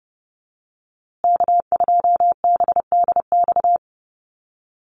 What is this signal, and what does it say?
Morse code, heard as a keyed tone, about 30 wpm, 700 Hz. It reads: K2BDX
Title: USA Call Signs Repeated 2x 30wpm